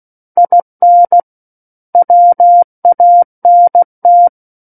Morse code, heard as a keyed tone, about 16 wpm, 700 Hz. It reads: IN WANT
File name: Book One / 07-Chapter 7 - 16 WPM